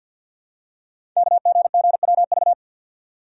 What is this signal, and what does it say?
Morse code, heard as a keyed tone, about 50 wpm, 690 Hz. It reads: KGGWV
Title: Code Groups - 5 Letters - 50wpm